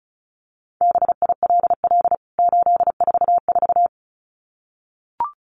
Morse code, heard as a keyed tone, about 34 wpm, 700 Hz. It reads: BILL 844
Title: CWT Exchange 35wpm